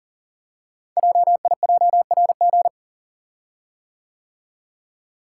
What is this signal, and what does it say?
Morse code, heard as a keyed tone, about 38 wpm, 690 Hz. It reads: JIJRG
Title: Code Groups - 5 Letters - 40wpm